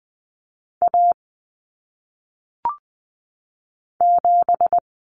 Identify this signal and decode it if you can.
Morse code, heard as a keyed tone, about 20 wpm, 700 Hz. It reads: A 7
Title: Common Cut Numbers - 1-Digit 20wpm